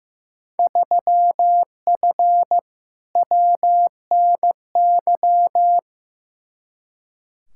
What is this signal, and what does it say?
Morse code, heard as a keyed tone, about 15 wpm, 700 Hz. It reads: 3F WNY
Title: Field Day Exchanges - 15wpm